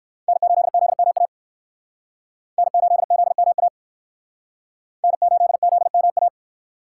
Morse code, heard as a keyed tone, about 58 wpm, 700 Hz. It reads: N8ZMA N8ZMA N8ZMA